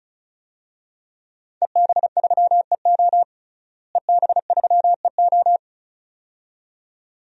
Morse code, heard as a keyed tone, about 36 wpm, 700 Hz. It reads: EB3EO EB3EO
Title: DXCC-100 Call Signs - 5 Characters - Repeated 2x - 35wpm